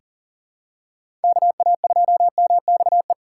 Morse code, heard as a keyed tone, about 40 wpm, 700 Hz. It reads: KA2MXE
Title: USA Call Signs - 6 Characters - 6x Character Spacing - 40wpm